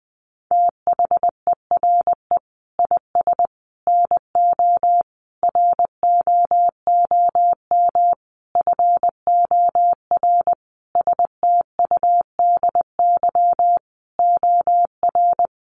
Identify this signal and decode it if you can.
Morse code, heard as a keyed tone, about 20 wpm, 700 Hz. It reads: THERE IS NO ROOM FOR STUDY OR
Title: Sentences from Top 1200 Words 20wpm